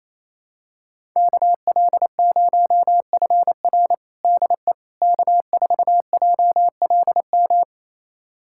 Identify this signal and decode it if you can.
Morse code, heard as a keyed tone, about 28 wpm, 700 Hz. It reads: KL0FR DE K4JLM
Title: USA Callsign DE Callsign - 5 Characters - 28wpm